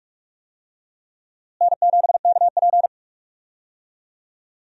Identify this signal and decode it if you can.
Morse code, heard as a keyed tone, about 46 wpm, 700 Hz. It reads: N7KP
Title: USA Call Signs - 4 Characters - Repeated 2x - 45wpm